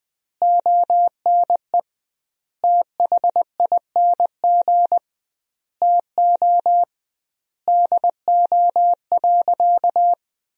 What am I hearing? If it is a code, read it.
Morse code, heard as a keyed tone, about 21 wpm, 700 Hz. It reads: ONE THING TO DO.
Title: 2x Word-Spacing - Sentences from Top 100 Words - 20wpm